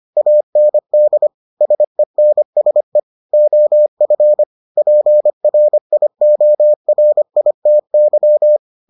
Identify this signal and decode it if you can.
Morse code, heard as a keyed tone, about 25 wpm, 600 Hz. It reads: AND SENSE OF PRIORITY